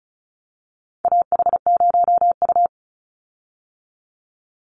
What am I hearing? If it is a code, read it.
Morse code, heard as a keyed tone, about 34 wpm, 700 Hz. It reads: AH0U
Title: Call Signs - 35wpm